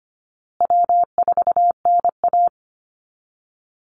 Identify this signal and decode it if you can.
Morse code, heard as a keyed tone, about 25 wpm, 700 Hz. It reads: W4NA
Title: Easy Call Signs Repeated 3x - 25wpm